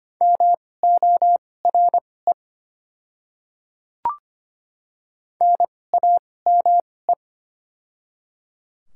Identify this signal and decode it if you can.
Morse code, heard as a keyed tone, about 25 wpm, spaced as for 18 wpm, 700 Hz. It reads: MORE NAME
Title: 2x Character Spacing - Top 500 Words - 25wpm